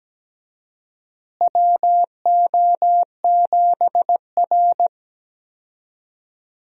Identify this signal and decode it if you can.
Morse code, heard as a keyed tone, about 17 wpm, 700 Hz. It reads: WO7R
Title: USA Call Signs - 4 Characters - Repeated 3x - 17wpm